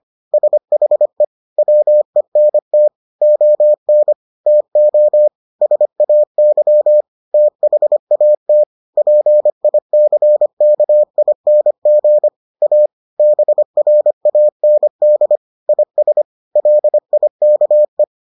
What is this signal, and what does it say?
Morse code, heard as a keyed tone, about 25 wpm, 600 Hz. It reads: SHE WENT ON TO SAY THAT PICKING A BRAND IS LIKE